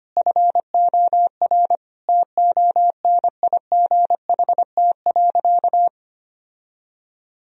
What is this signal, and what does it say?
Morse code, heard as a keyed tone, about 25 wpm, 700 Hz. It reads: FOR TONIGHT.
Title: Sentences from Top 1000 Words - Review Part 3 - 25wpm